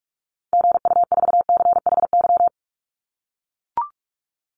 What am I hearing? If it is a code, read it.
Morse code, heard as a keyed tone, about 44 wpm, 700 Hz. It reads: GU4CHY
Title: Call Signs - 45wpm